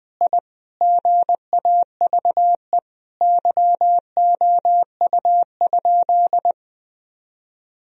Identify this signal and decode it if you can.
Morse code, heard as a keyed tone, about 20 wpm, 700 Hz. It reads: I GAVE YOU?
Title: Sentences from Top 1000 Words - Review Part 4 - 20wpm